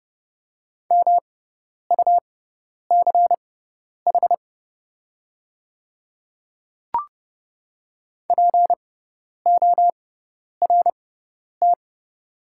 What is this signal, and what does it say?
Morse code, heard as a keyed tone, about 29 wpm, spaced as for 10 wpm, 700 Hz. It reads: MUCH PORT
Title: 6x Character Spacing - Top 200 Words - 30wpm